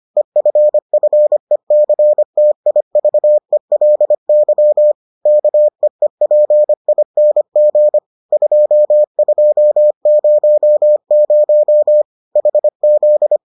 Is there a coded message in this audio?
Yes: Morse code from 0.2 to 13.4 s, heard as a keyed tone, about 25 wpm, 600 Hz. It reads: EFFECTIVELY KEEPING 2200 HZ